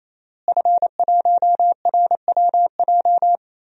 Morse code, heard as a keyed tone, about 28 wpm, 700 Hz. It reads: F1RWJ
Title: DXCC-100 Call Signs - 5 Characters - Repeated 3x - 28wpm